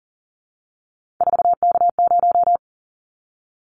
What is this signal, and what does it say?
Morse code, heard as a keyed tone, about 40 wpm, 700 Hz. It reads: 4K0
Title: Callsign Prefixes for Top 100 DXCC - 40wpm